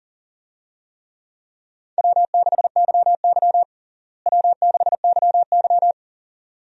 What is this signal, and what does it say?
Morse code, heard as a keyed tone, about 40 wpm, 700 Hz. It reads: W6YY W6YY
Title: USA Call Signs - 4 Characters - Repeated 2x - 40wpm